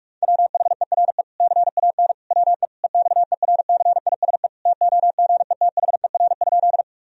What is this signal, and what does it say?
Morse code, heard as a keyed tone, about 47 wpm, 700 Hz. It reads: WHERE CAN WE EXERCISE TOGETHER?
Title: Sentences from Top 1000 Words - Review Part 4 - 45wpm